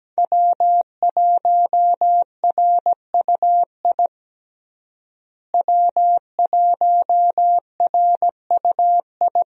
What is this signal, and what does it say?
Morse code, heard as a keyed tone, about 17 wpm, 700 Hz. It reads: W1RUI W1RUI
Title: USA Call Signs - 5 Characters - Repeated 2x - 17wpm